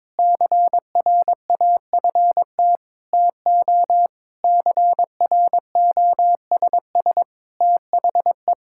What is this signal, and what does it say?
Morse code, heard as a keyed tone, about 22 wpm, 700 Hz. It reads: CRAFT TO CROSS THE
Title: Sentences from Top 1000 Words - Review Part 6 - 22wpm